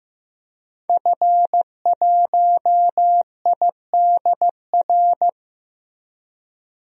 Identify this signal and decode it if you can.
Morse code, heard as a keyed tone, about 15 wpm, 700 Hz. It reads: F1IDR
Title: DXCC-100 Call Signs - 5 Characters - Repeated 3x - 15wpm